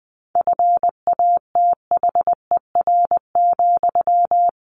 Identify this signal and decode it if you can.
Morse code, heard as a keyed tone, about 20 wpm, 700 Hz. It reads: FATHER,
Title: Sentences from Top 1100 Words 20wpm